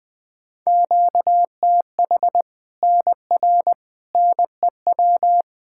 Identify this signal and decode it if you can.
Morse code, heard as a keyed tone, about 20 wpm, 700 Hz. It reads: QTH NR NEW
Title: Practice QSO - Exchange 1 - Beginning - 20wpm